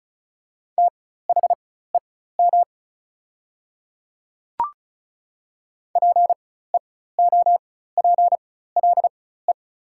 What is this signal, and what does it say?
Morse code, heard as a keyed tone, about 34 wpm, spaced as for 16 wpm, 690 Hz. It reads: THEM PEOPLE
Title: 4x Character Spacing - Top 300 Words - 35wpm